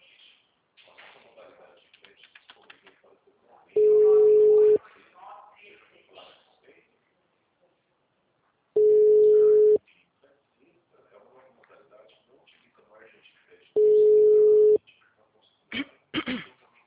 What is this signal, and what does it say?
Ringback tone (Europe, 425 Hz), with speech.